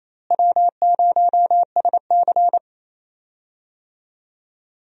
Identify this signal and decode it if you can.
Morse code, heard as a keyed tone, about 27 wpm, 700 Hz. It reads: W0SC
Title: USA Call Signs - 4 Characters - Repeated 3x - 28wpm